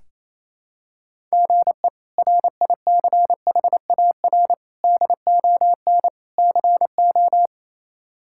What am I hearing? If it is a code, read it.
Morse code, heard as a keyed tone, about 28 wpm, 700 Hz. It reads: GE RICHAR DON CO
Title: K1USN SST QSO Practice - 28wpm